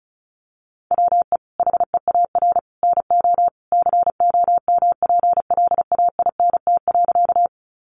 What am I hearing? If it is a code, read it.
Morse code, heard as a keyed tone, about 34 wpm, 700 Hz. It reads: WE HEAR NO COMPLAINT.
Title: Sentences from Top 1400 Words 35wpm